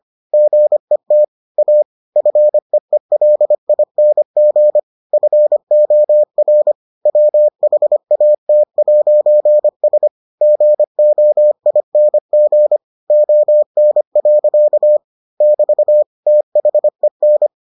Morse code, heard as a keyed tone, about 25 wpm, 600 Hz. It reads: GET A FEELING FOR WHAT'S GOING ON. <BT> THEN